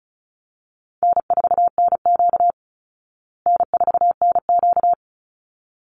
Morse code, heard as a keyed tone, about 34 wpm, 700 Hz. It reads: N4NQ N4NQ
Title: Easy Call Signs Repeated 2x - 35wpm